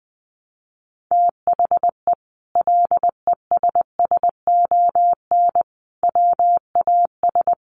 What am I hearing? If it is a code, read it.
Morse code, heard as a keyed tone, about 19 wpm, 700 Hz. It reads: THE LESSON WAS
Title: Sentences from Top 1200 Words 20wpm